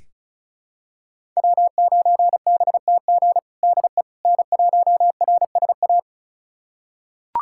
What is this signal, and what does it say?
Morse code, heard as a keyed tone, about 36 wpm, 700 Hz. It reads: W9BTG DE N1RSA
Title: USA Callsign DE Callsign - 5 Characters - 35wpm